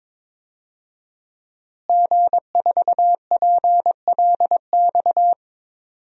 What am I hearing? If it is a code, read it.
Morse code, heard as a keyed tone, about 22 wpm, 700 Hz. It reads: G4PLX
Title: DXCC-100 Call Signs - 5 Characters - Repeated 3x - 22wpm